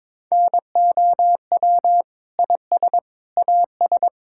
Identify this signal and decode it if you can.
Morse code, heard as a keyed tone, about 22 wpm, 700 Hz. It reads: NOW IS AS
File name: Sentences with 7-Word Phrases - 22wpm